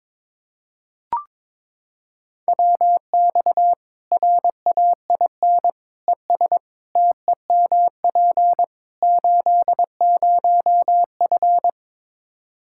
Morse code, heard as a keyed tone, about 22 wpm, 700 Hz. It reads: WX RAIN ES TEMP 80F
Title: Practice QSO - Exchange 2 - Info - 22wpm